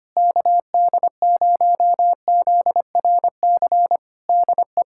Morse code, heard as a keyed tone, about 25 wpm, 700 Hz. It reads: KD0ZRC DE
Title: K1USN SST QSO Practice 25wpm